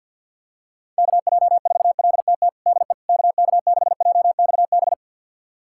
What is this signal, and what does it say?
Morse code, heard as a keyed tone, about 53 wpm, 700 Hz. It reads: KJ4LTT DE KK6JXB
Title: USA Callsign DE Callsign 50wpm